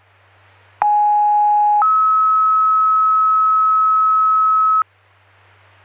Two-tone page, heard as a one-second tone, then a three-second tone, about 830 then 1290 Hz. It is heard from 0.8 to 4.8 s.